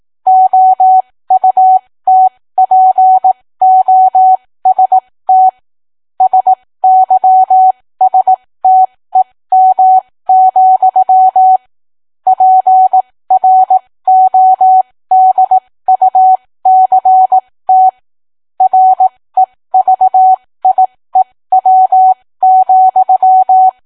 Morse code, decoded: OUTPOST SYSTEM, PRODUCT REVIEW,